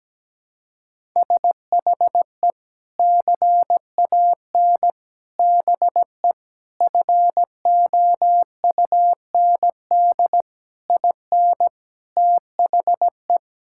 Morse code, heard as a keyed tone, about 17 wpm, 700 Hz. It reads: SHE CAN BE FOUND IN THE
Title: Sentences from Top 1000 Words - Review Part 6 - 17wpm